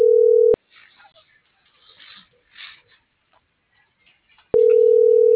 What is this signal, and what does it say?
Ringback tone (North America), 440 + 480 Hz.
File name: CENTRAL NAVIERAS / out leonardo.pe.a